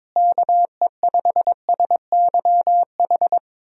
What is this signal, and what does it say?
Morse code, heard as a keyed tone, about 22 wpm, 700 Hz. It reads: KE5SYH
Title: USA Call Signs - 6 Characters - Repeated 2x - 22wpm